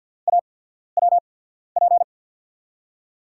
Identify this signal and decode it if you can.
Morse code, heard as a keyed tone, about 47 wpm, spaced as for 13 wpm, 700 Hz. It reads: AWP
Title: USA Call Signs - 5 Characters - 8x Character Spacing - 50wpm